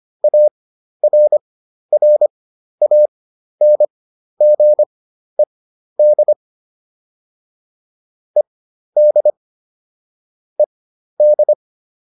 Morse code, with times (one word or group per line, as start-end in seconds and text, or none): 0.2-6.3 ARRANGED
8.4-9.3 ED
10.6-11.5 ED